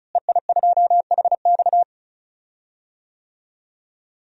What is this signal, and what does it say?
Morse code, heard as a keyed tone, about 34 wpm, 700 Hz. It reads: EI2HX
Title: DXCC-100 Call Signs - 5 Characters - Repeated 2x - 35wpm